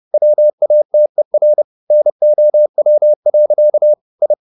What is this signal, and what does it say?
Morse code, heard as a keyed tone, about 30 wpm, 600 Hz. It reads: WATER NOW. I